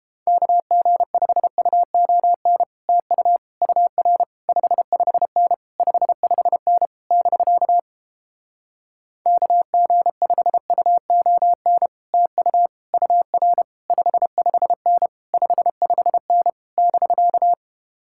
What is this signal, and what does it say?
Morse code, heard as a keyed tone, about 29 wpm, 700 Hz. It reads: KG5UON TU UR 55N 55N <BK> KG5UON TU UR 55N 55N <BK>